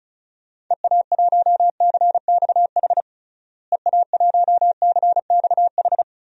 Morse code, heard as a keyed tone, about 36 wpm, 700 Hz. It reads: EA1CXH EA1CXH
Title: DXCC-100 Call Signs - 6 Characters - Repeated 3x - 35wpm